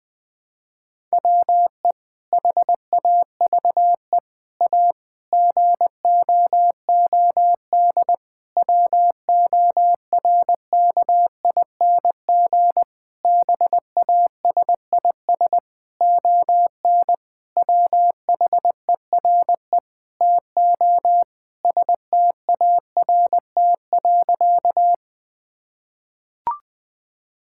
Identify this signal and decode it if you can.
Morse code, heard as a keyed tone, about 20 wpm, 700 Hz. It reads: WE HAVE A GOOD WORKING BASIS ON WHERE TO START.
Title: Sentences from Top 1000 Words - Review Part 5 - 20wpm